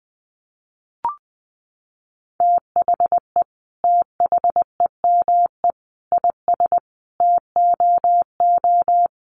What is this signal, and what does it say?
Morse code, heard as a keyed tone, about 20 wpm, 700 Hz. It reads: THE THEME IS TOO
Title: Sentences from Top 1300 Words 20wpm